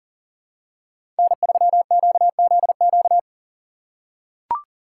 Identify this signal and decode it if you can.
Morse code, heard as a keyed tone, about 40 wpm, 700 Hz. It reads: N3QZQ
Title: USA Call Signs - 5 Characters - 3x Character Spacing - 40wpm